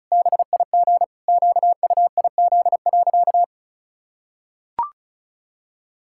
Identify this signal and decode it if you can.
Morse code, heard as a keyed tone, about 36 wpm, 700 Hz. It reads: BIG QUIZ.
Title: Pangram Challenge v2 - 35wpm